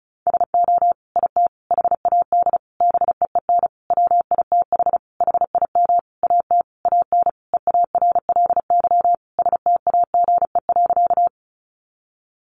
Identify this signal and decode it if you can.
Morse code, heard as a keyed tone, about 34 wpm, 700 Hz. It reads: SO IT HAD BEEN WITH HIM AT AN EARLY STAGE.
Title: Sentences from Top 1200 Words 35wpm